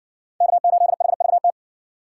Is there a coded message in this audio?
Yes: Morse code from 0.4 to 1.5 s, heard as a keyed tone, about 60 wpm, 700 Hz. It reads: K8HVT